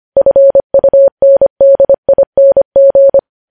Morse code, heard as a keyed tone, about 25 wpm, 550 Hz. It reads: FUNDING
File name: podcast / News headlines-1x-25x25